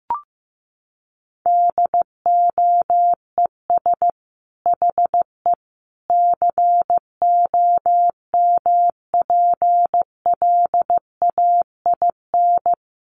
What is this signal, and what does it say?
Morse code, heard as a keyed tone, about 15 wpm, 700 Hz. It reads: DOES HE COMPLAIN
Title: Sentences from Top 1300 Words 15wpm